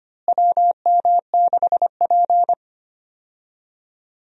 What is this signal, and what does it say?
Morse code, heard as a keyed tone, about 25 wpm, 700 Hz. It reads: WM6P
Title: USA Call Signs - 4 Characters - Repeated 3x - 25wpm